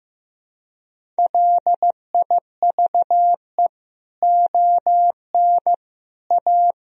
Morse code, heard as a keyed tone, about 16 wpm, 700 Hz. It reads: LIVE ON A
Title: Sentences from Top 1000 Words - Review Part 1 - 15wpm